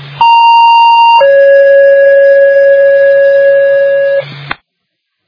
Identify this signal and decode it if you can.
Two-tone page, heard as a one-second tone, then a three-second tone, about 930 then 560 Hz. It is heard from 0.2 to 4.2 s.